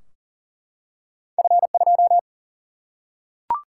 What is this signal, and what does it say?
Morse code, heard as a keyed tone, about 40 wpm, 700 Hz. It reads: F2
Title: Single-Letter Single-Number 40wpm